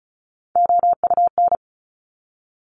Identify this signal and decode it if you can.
Morse code, heard as a keyed tone, about 36 wpm, 700 Hz. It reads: OUN
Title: Top Three Letter Combinations in Words 35wpm